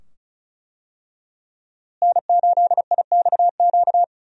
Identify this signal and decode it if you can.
Morse code, heard as a keyed tone, about 36 wpm, 700 Hz. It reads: N8IXQ
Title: USA Call Signs - 5 Characters - Repeated 2x - 35wpm